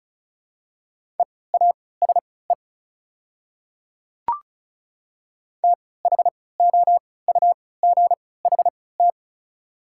Morse code, heard as a keyed tone, about 34 wpm, spaced as for 20 wpm, 700 Hz. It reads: EASE THOUGHT
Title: 3x Character Spacing - Top 300 Words - 35wpm